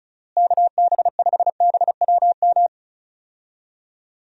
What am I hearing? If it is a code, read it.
Morse code, heard as a keyed tone, about 34 wpm, 700 Hz. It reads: KB5BWM
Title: USA Call Signs - 6 Characters - Repeated 2x - 35wpm